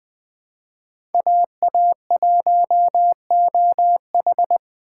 Morse code, heard as a keyed tone, about 20 wpm, 700 Hz. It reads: AA1OH
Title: USA Call Signs - 2x Character Spacing - 20wpm